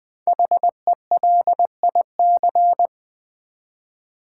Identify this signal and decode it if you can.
Morse code, heard as a keyed tone, about 20 wpm, 700 Hz. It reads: HELIC
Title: Build A Word - 5 Letters At A Time - 10-Letter Words - 20wpm